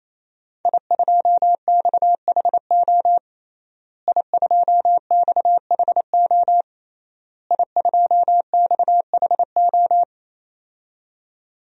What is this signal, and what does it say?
Morse code, heard as a keyed tone, about 27 wpm, 700 Hz. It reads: I2XHO I2XHO I2XHO
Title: DXCC-100 Call Signs - 5 Characters - Repeated 3x - 28wpm